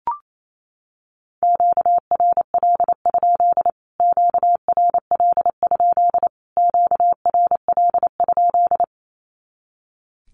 Morse code, heard as a keyed tone, about 28 wpm, 700 Hz. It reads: QRL? QRL? QRL?
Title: Q-Codes Repeated 3x 28wpm